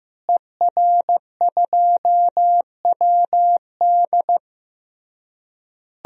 Morse code, heard as a keyed tone, about 15 wpm, 700 Hz. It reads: ER2WD
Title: DXCC-100 Call Signs - 5 Characters - Repeated 3x - 15wpm